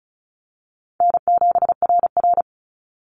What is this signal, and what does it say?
Morse code, heard as a keyed tone, about 36 wpm, 700 Hz. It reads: N7RR
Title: NA SOTA Activator Callsigns 35wpm